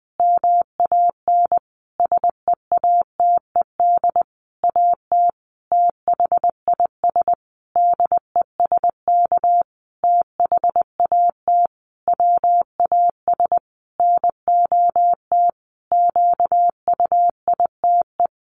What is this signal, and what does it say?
Morse code, heard as a keyed tone, about 20 wpm, 700 Hz. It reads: MAN SEATED AT THIS DESK THAT WAS NOT QUITE
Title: A Sound Of Thunder 20wpm